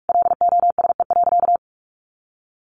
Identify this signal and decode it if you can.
Morse code, heard as a keyed tone, about 44 wpm, 700 Hz. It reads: LOSE.
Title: Sentences from Top 1100 Words 45wpm